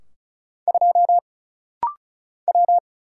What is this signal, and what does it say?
Morse code, heard as a keyed tone, about 34 wpm, 700 Hz. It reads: 2 W
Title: Single Letter-Number - Rapid-Fire 35wpm